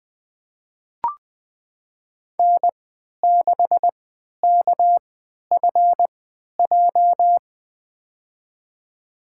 Morse code, heard as a keyed tone, about 21 wpm, spaced as for 11 wpm, 700 Hz. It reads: N6KFJ
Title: USA Call Signs - 3x Character Spacing - 20wpm